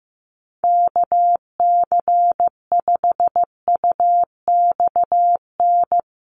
Morse code, heard as a keyed tone, about 15 wpm, 700 Hz. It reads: KC5UXN
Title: USA Call Signs Repeated 2x 15wpm